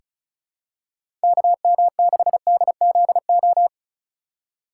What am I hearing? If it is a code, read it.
Morse code, heard as a keyed tone, about 36 wpm, 700 Hz. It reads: KM6DZO
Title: USA Call Signs - 3x Character Spacing - 35wpm